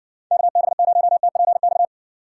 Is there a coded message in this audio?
Yes: Morse code from 0.3 to 1.9 s, heard as a keyed tone, about 60 wpm, 700 Hz. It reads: KB0TPX